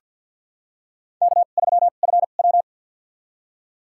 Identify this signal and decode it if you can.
Morse code, heard as a keyed tone, about 47 wpm, spaced as for 35 wpm, 700 Hz. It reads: K3FW
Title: USA Call Signs - 4 Characters - 2x Character Spacing - 50wpm